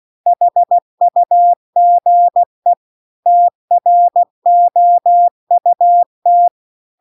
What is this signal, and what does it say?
Morse code, heard as a keyed tone, about 16 wpm, 700 Hz. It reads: HUGE TROUT